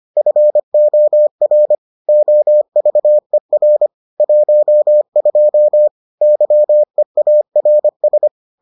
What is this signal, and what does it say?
Morse code, heard as a keyed tone, about 25 wpm, 600 Hz. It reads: FOR OVER 12 YEARS